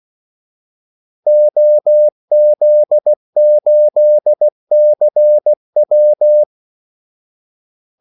Morse code, heard as a keyed tone, about 16 wpm, 600 Hz. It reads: OZ8CW
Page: Kaldesignaler 16 wpm